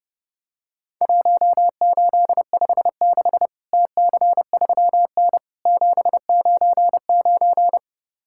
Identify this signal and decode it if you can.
Morse code, heard as a keyed tone, about 32 wpm, 700 Hz. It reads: 1856 TC3N 799
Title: Worked All Europe DX Contest - QTC Exchange - 30wpm